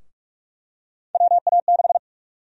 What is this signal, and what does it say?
Morse code, heard as a keyed tone, about 41 wpm, 700 Hz. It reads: WA6
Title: USA Call Sign Prefixes - 45wpm